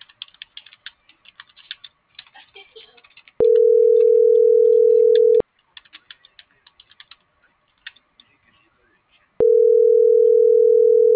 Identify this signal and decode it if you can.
Ringback tone (North America), 440 + 480 Hz.
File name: CENTRAL SOTOVA / out ricardo.lopez